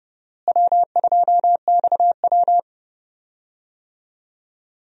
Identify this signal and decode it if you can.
Morse code, heard as a keyed tone, about 31 wpm, 700 Hz. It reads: W2XW
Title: USA Call Signs - 4 Characters - Repeated 3x - 30wpm